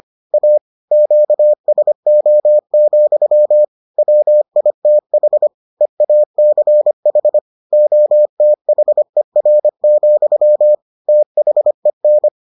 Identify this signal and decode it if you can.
Morse code, heard as a keyed tone, about 25 wpm, 600 Hz. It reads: A QSO, WITH EACH OTHER, THEN